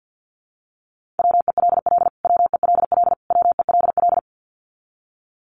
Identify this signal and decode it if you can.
Morse code, heard as a keyed tone, about 50 wpm, 700 Hz. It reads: WELL WELL WELL
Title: Top 300 Words Repeated 50wpm